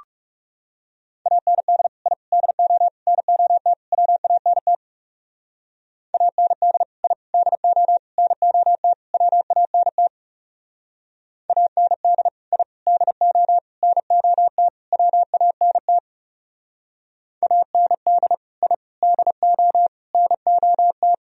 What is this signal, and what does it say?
Morse code, heard as a keyed tone, about 36 wpm, 700 Hz. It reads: AND I DO NOT WANT AND I DO NOT WANT AND I DO NOT WANT AND I DO NOT
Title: Speed-Racing - Sets of 5 Words 30wpm